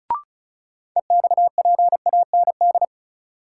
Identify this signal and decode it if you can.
Morse code, heard as a keyed tone, about 36 wpm, 700 Hz. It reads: EXPAND
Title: Sentences from Top 1400 Words - New Word Drill Alt - 35wpm